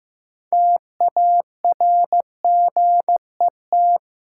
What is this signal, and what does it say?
Morse code, heard as a keyed tone, about 15 wpm, 700 Hz. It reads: TARGET
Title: Sentences from Top 1200 Words - New Word Drill Alt - 15wpm